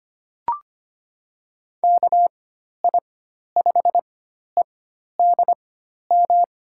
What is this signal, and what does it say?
Morse code, heard as a keyed tone, about 25 wpm, spaced as for 12 wpm, 700 Hz. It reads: KI5EDM
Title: USA Call Signs - 4x Character Spacing - 25wpm